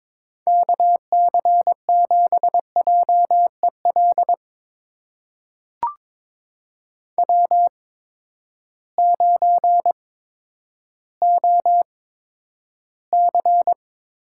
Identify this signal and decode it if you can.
Morse code, heard as a keyed tone, about 22 wpm, spaced as for 11 wpm, 700 Hz. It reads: KC7JEL W9OC
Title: USA Call Signs - 8x Character Spacing - 22wpm